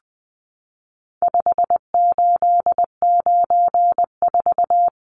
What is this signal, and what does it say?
Morse code, heard as a keyed tone, about 20 wpm, 700 Hz. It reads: 5894
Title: Numbers - 4-Digit 20wpm